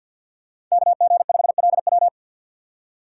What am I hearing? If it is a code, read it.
Morse code, heard as a keyed tone, about 51 wpm, 700 Hz. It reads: KG5LW
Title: USA Call Signs - 5 Characters - 4x Character Spacing - 50wpm